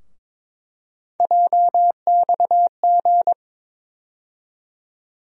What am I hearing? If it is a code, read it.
Morse code, heard as a keyed tone, about 22 wpm, 700 Hz. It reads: JXG
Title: USA Call Sign Postfixes - 3 Letters 22wpm